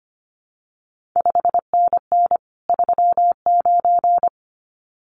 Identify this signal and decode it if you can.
Morse code, heard as a keyed tone, about 25 wpm, 700 Hz. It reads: 5NN 39
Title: CQ WW DX Contest Exchange - 25wpm